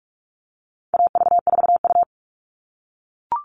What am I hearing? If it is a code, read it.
Morse code, heard as a keyed tone, about 44 wpm, 700 Hz. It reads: AV4U
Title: All Cut Numbers - 4-Digit 45wpm